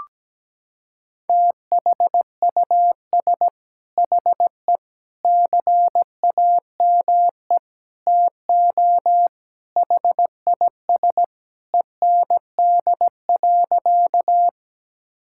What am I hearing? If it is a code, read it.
Morse code, heard as a keyed tone, about 17 wpm, 700 Hz. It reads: THUS HE CAME TO HIS END.
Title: Sentences from Top 1000 Words - Review Part 5 - 17wpm